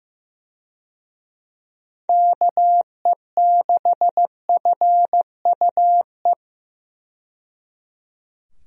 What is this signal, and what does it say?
Morse code, heard as a keyed tone, about 15 wpm, 700 Hz. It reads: KE6FUE
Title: USA Call Signs - 6 Characters - 15wpm